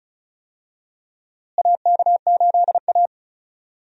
Morse code, heard as a keyed tone, about 36 wpm, 700 Hz. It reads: AK8A
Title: USA Call Signs - 4 Characters - Repeated 3x - 35wpm